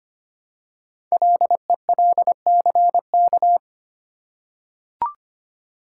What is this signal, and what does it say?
Morse code, heard as a keyed tone, about 25 wpm, 700 Hz. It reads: LELCK
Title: Code Groups - 5 Letters - 25wpm